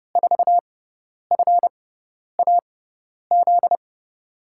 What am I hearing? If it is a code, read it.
Morse code, heard as a keyed tone, about 32 wpm, spaced as for 10 wpm, 700 Hz. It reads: 4FAZ
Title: USA Call Signs - 6 Characters - 6x Character Spacing - 30wpm